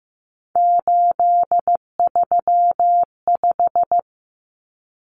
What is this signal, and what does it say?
Morse code, heard as a keyed tone, about 15 wpm, 700 Hz. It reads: 835
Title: Numbers - 3-Digit 15wpm